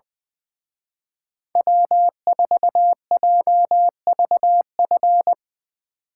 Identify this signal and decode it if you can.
Morse code, heard as a keyed tone, about 22 wpm, 700 Hz. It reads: W4JVF
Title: Speed Racing - USA Call Signs 20wpm